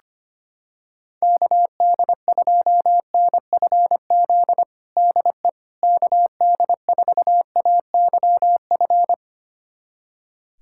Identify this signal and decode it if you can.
Morse code, heard as a keyed tone, about 25 wpm, 700 Hz. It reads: KD2NFZ DE KD4AYF
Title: USA Callsign DE Callsign - 6 Characters - 25wpm